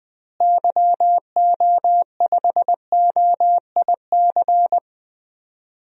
Morse code, heard as a keyed tone, about 20 wpm, 700 Hz. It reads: YO5OIC
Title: DXCC-100 Call Signs - 6 Characters - Repeated 2x - 20wpm